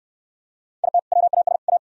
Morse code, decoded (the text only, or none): IT WAS A